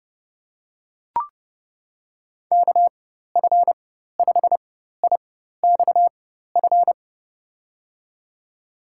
Morse code, heard as a keyed tone, about 30 wpm, spaced as for 14 wpm, 700 Hz. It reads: KF5IXF
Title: USA Call Signs - 6 Characters - 4x Character Spacing - 30wpm